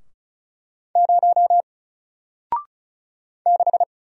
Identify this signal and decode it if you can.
Morse code, heard as a keyed tone, about 36 wpm, 700 Hz. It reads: 0 6
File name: Single-Letter-Number-35wpm